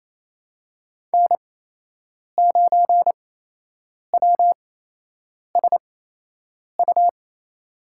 Morse code, heard as a keyed tone, about 28 wpm, spaced as for 8 wpm, 700 Hz. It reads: N9WSU
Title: USA Call Signs - 5 Characters - 8x Character Spacing - 28wpm